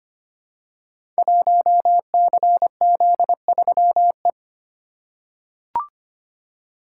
Morse code, heard as a keyed tone, about 25 wpm, 700 Hz. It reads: 1CZ3E
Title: Code Groups - 5 Letter:Numbers - 25wpm